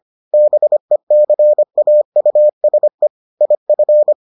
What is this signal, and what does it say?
Morse code, heard as a keyed tone, about 25 wpm, 600 Hz. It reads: BECAUSE IF